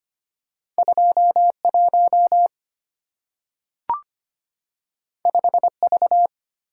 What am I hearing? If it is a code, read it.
Morse code, heard as a keyed tone, about 25 wpm, 700 Hz. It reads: 21 5V
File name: Two Letter-Number Combinations - 25wpm